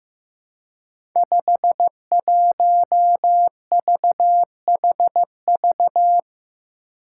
Morse code, heard as a keyed tone, about 15 wpm, 700 Hz. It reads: 51VHV
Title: Five Challenging Letter-Number Combinations 15wpm